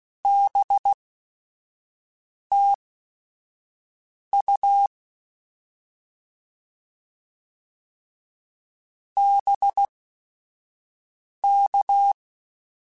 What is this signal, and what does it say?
Morse code, heard as a keyed tone, about 16 wpm, spaced as for 5 wpm, 800 Hz. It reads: BTU BK
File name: Short Runs / Short Runs-mp3s / Words @5